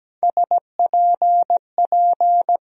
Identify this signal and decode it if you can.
Morse code, heard as a keyed tone, about 17 wpm, 700 Hz. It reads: SPP
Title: USA Call Sign Postfixes - 3 Letters 17wpm